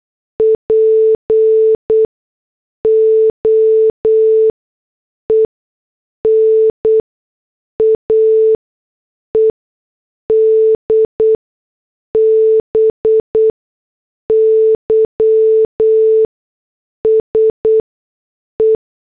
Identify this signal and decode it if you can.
Morse code, heard as a keyed tone, about 8 wpm, 440 Hz. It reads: P O E N A E D B Y S E
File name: morse / headlines